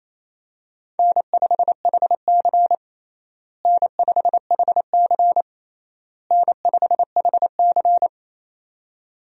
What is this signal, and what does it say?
Morse code, heard as a keyed tone, about 28 wpm, 700 Hz. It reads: N5HC N5HC N5HC
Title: USA Call Signs - 4 Characters - Repeated 3x - 28wpm